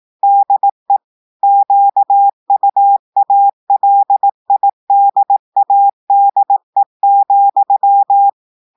Morse code, heard as a keyed tone, about 18 wpm, 800 Hz. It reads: DE QUALIDADE,
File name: portugais texte en clair 2 vitesse 18 mots minutes